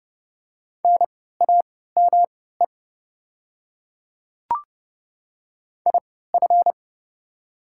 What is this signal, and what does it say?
Morse code, heard as a keyed tone, about 30 wpm, spaced as for 17 wpm, 690 Hz. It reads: NAME IF
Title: 3x Character Spacing - Top 200 Words - 30wpm